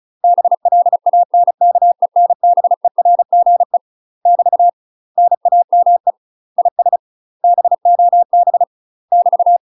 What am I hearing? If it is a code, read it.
Morse code, heard as a keyed tone, about 36 wpm, 700 Hz. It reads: BLANKENBERGE <BT> NAME IS BOB <BT>
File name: UK MASTER QSOs TRAINING / 35WPM / UK MASTER QSOs 0006